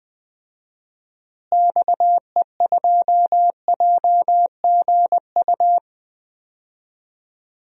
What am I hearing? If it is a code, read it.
Morse code, heard as a keyed tone, about 20 wpm, 700 Hz. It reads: XE2JGU
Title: DXCC-100 Call Signs - 6 Characters - Repeated 2x - 20wpm